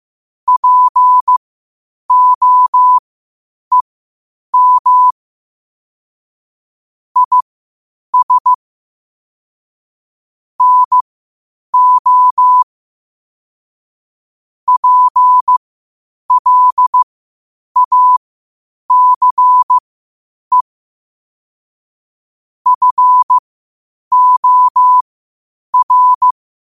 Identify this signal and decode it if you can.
Morse code, heard as a keyed tone, about 15 wpm, spaced as for 8 wpm, 1000 Hz. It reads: POEM IS NO PLACE FOR
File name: qotdfiles / 8wpm